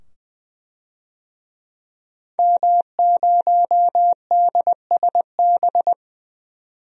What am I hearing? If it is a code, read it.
Morse code, heard as a keyed tone, about 20 wpm, 700 Hz. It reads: M0DSB
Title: DXCC-100 Call Signs - 5 Characters - 20wpm